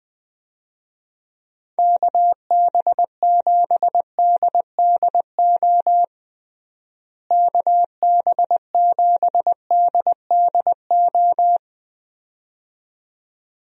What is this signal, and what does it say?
Morse code, heard as a keyed tone, about 21 wpm, 700 Hz. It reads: KB7DDO KB7DDO
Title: USA Call Signs - 6 Characters - Repeated 2x - 20wpm